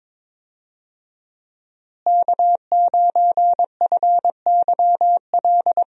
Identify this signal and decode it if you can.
Morse code, heard as a keyed tone, about 21 wpm, 700 Hz. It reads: K9FYL
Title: USA Call Signs - 5 Characters - Repeated 2x - 22wpm